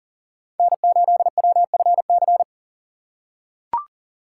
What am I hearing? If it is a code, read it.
Morse code, heard as a keyed tone, about 40 wpm, 700 Hz. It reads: N8WFC
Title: USA Call Signs - 2x Character Spacing - 40wpm